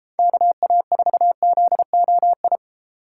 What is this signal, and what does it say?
Morse code, heard as a keyed tone, about 33 wpm, 700 Hz. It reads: KA4ZOI